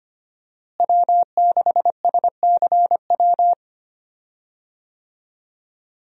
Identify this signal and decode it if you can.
Morse code, heard as a keyed tone, about 25 wpm, 700 Hz. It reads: W6SCW
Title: USA Call Signs - 5 Characters - Repeated 2x - 25wpm